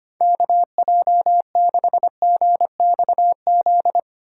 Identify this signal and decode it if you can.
Morse code, heard as a keyed tone, about 25 wpm, 700 Hz. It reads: KJ6GXZ
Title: USA Call Signs - 3x Character Spacing - 25wpm